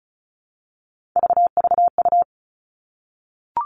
Morse code, heard as a keyed tone, about 36 wpm, 700 Hz. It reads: VVU
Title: Three Challenging Letter-Number Combinations 35wpm